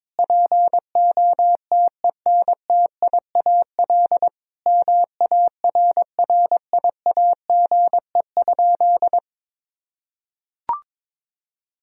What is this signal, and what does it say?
Morse code, heard as a keyed tone, about 22 wpm, 700 Hz. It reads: POTENTIAL MARRIAGE?
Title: Sentences from Top 1000 Words - Review Part 4 - 22wpm